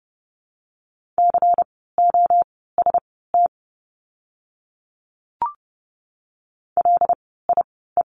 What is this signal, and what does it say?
Morse code, heard as a keyed tone, about 30 wpm, spaced as for 17 wpm, 700 Hz. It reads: COST LIE
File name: Lesson 16f - Letter c - Words 30wpm